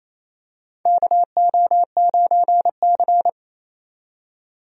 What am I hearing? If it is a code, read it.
Morse code, heard as a keyed tone, about 28 wpm, 700 Hz. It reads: KO9C
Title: USA Call Signs - 4 Characters - 28wpm